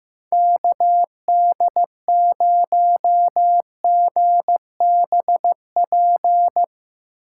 Morse code, heard as a keyed tone, about 15 wpm, 700 Hz. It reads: KD0GBP
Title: USA Call Signs - 6x Character Spacing - 15wpm